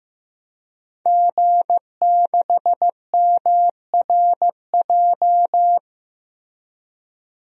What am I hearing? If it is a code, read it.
Morse code, heard as a keyed tone, about 15 wpm, 700 Hz. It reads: G6MRJ
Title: DXCC-100 Call Signs - 5 Characters - Repeated 3x - 15wpm